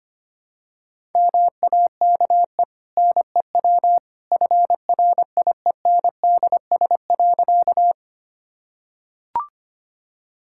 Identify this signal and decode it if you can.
Morse code, heard as a keyed tone, about 25 wpm, 690 Hz. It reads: MAKE NEW FRIENDS.
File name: Easy Sentences with Hints - 25wpm